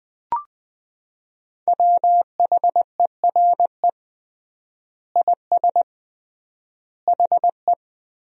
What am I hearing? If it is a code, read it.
Morse code, heard as a keyed tone, about 20 wpm, 700 Hz. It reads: WHERE IS HE
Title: 3x Word-Spacing - Sentences from Top 200 Words - 20wpm